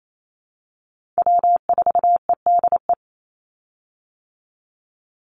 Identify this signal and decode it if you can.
Morse code, heard as a keyed tone, about 28 wpm, 700 Hz. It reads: W4EDE
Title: Call Signs - 28wpm